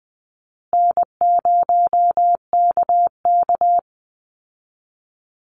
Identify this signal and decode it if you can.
Morse code, heard as a keyed tone, about 20 wpm, 700 Hz. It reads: N0KK
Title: Easy Call Signs Repeated 3x - 20wpm